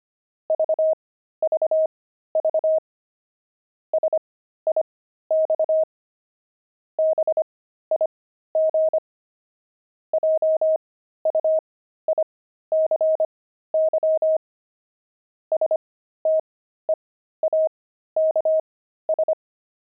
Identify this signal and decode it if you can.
Morse code, heard as a keyed tone, about 25 wpm, spaced as for 13 wpm, 630 Hz. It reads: VVV SIX BIG JUICY STEAKS
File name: pangram-fs-13-cs-25-steaks